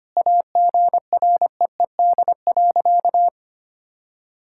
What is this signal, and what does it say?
Morse code, heard as a keyed tone, about 25 wpm, 700 Hz. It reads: AGREED.
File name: practice / 25wpm / Sentences with 3-Word Phrases - 25wpm